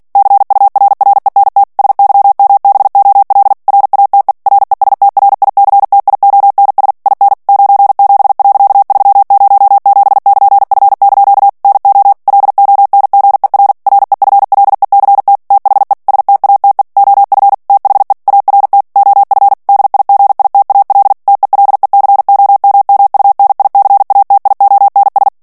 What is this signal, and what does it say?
Morse code, heard as a keyed tone, about 48 wpm, 800 Hz. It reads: CURRENT SYMBOL RATE RESTRICTIONS IN 97.307<KN>F) NO LONGER REFLECT THE STATE OF THE ART OF DIGITAL TELECOMMUNICATIONS
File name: ANGLAIS texte clair 3 vitesse 45 mots minute